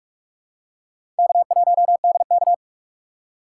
Morse code, heard as a keyed tone, about 46 wpm, 700 Hz. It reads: K1DK
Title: USA Call Signs - 2x Character Spacing - 45wpm